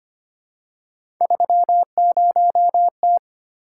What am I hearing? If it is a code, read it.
Morse code, heard as a keyed tone, about 25 wpm, 700 Hz. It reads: 30T
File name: Three Letter-Number Combinations - 25wpm